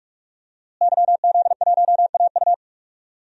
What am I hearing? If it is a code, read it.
Morse code, heard as a keyed tone, about 44 wpm, 700 Hz. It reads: YZ1AU
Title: DXCC-100 Call Signs - 5 Characters - Repeated 2x - 45wpm